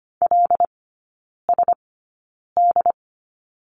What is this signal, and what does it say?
Morse code, heard as a keyed tone, about 25 wpm, spaced as for 9 wpm, 700 Hz. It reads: LSD
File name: Lesson 14e - Number 2 - Three Characters 25wpm